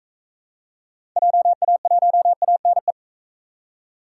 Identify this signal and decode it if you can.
Morse code, heard as a keyed tone, about 41 wpm, 700 Hz. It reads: JA1ANE